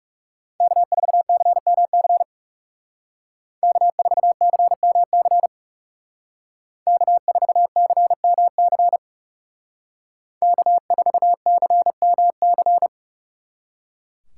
Morse code, heard as a keyed tone, about 36 wpm, 700 Hz. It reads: K4CMC K4CMC K4CMC K4CMC
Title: Speed Racing - USA 1x3 Call Signs 30wpm